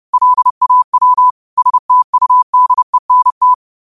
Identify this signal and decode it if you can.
Morse code, heard as a keyed tone, about 30 wpm, 1000 Hz. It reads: LAW STUDENT
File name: qotdfiles / 30wpm